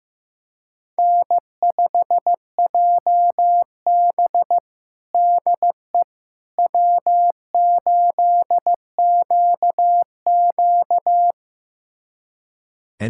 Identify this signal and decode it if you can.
Morse code, heard as a keyed tone, about 15 wpm, 700 Hz. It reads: N5JB DE W8QQ
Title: USA Callsign DE Callsign - 4 Characters - 15wpm